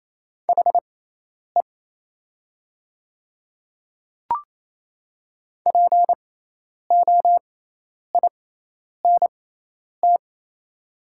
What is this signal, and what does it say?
Morse code, heard as a keyed tone, about 28 wpm, spaced as for 10 wpm, 700 Hz. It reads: HE POINT
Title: 6x Character Spacing - Top 200 Words - 28wpm